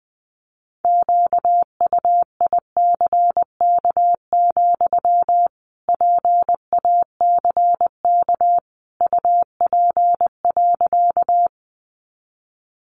Morse code, heard as a keyed tone, about 20 wpm, 700 Hz. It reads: QUICK, PACK UP.
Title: Sentences from Top 1300 Words 20wpm